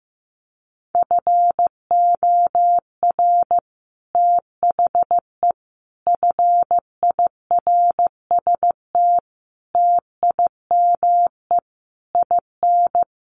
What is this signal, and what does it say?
Morse code, heard as a keyed tone, about 15 wpm, 700 Hz. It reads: FOR THE FIRST TIME IN
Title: Sets of 5 Words 15wpm